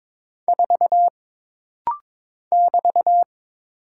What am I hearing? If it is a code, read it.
Morse code, heard as a keyed tone, about 22 wpm, 700 Hz. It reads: 4 <BT>
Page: This rapid-fire timing is geared to push your ICR (Instant Character Recognition) skills.